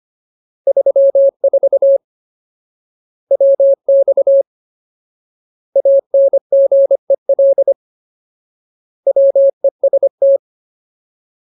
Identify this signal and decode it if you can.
Morse code, heard as a keyed tone, about 25 wpm, 550 Hz. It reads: 34 WX ANGEL WEST
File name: Ultimate Word List / 25WPM Ultimate